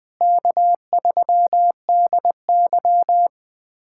Morse code, heard as a keyed tone, about 20 wpm, 700 Hz. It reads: K3DY
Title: USA Callsign DE Callsign - 4 Characters - 20wpm